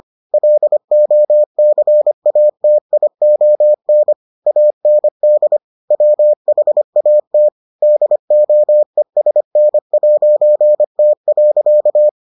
Morse code, heard as a keyed tone, about 25 wpm, 600 Hz. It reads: LOCATION AND WHAT DOESN'T.